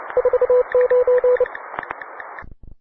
Morse code, heard as a keyed tone, about 29 wpm, 500 Hz, with radio noise. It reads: 49